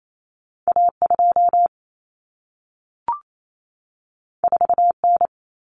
Morse code, heard as a keyed tone, about 28 wpm, 700 Hz. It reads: A2 4N
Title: Common Cut Numbers - 2-Digit 28wpm